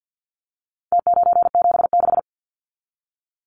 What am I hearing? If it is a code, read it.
Morse code, heard as a keyed tone, about 50 wpm, 700 Hz. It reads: T976
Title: Numbers - Contest Style - 4-Digit 50wpm